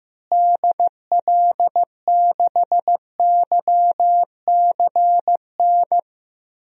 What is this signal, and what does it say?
Morse code, heard as a keyed tone, about 15 wpm, 700 Hz. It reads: DL6YCN
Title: DXCC-100 Call Signs - 6 Characters - Repeated 3x - 15wpm